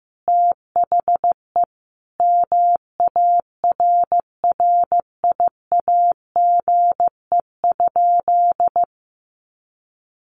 Morse code, heard as a keyed tone, about 15 wpm, 700 Hz. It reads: THE MARRIAGE?
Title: Sentences from Top 1200 Words 15wpm